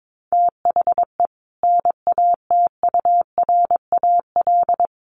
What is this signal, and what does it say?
Morse code, heard as a keyed tone, about 22 wpm, 700 Hz. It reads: THE NATURAL
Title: Sentences from Top 1100 Words 22wpm